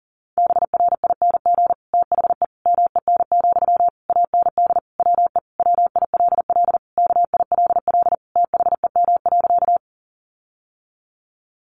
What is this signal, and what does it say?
Morse code, heard as a keyed tone, about 40 wpm, 700 Hz. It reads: BRING THE MEN, AND WE WILL KILL THEM.
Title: Sentences from Top 1400 Words 40wpm